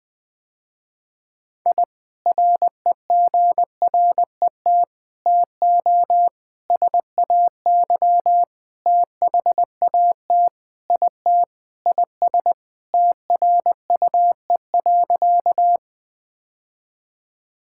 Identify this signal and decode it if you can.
Morse code, heard as a keyed tone, about 19 wpm, 700 Hz. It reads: I REGRET TO SAY THAT IT IS TRUE.
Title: Sentences with 3-Word Phrases - Spoken Beginning - 20wpm